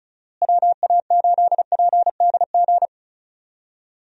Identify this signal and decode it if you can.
Morse code, heard as a keyed tone, about 34 wpm, 700 Hz. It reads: WA8PDG
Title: USA Call Signs - 6 Characters - Repeated 3x - 35wpm